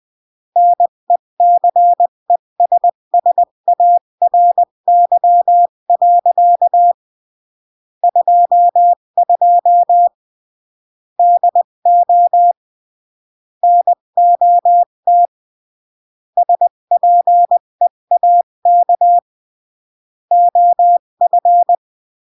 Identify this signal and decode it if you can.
Morse code, heard as a keyed tone, about 20 wpm, 700 Hz. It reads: NECESSARY. 22 DO NOT SPEAK OF
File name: CW / jodorowsky / 82 commandments 20-15